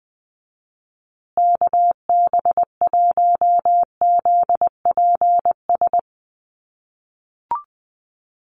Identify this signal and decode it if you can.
Morse code, heard as a keyed tone, about 20 wpm, 700 Hz. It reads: KB1ZPS
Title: USA Call Signs Repeated 2x 20wpm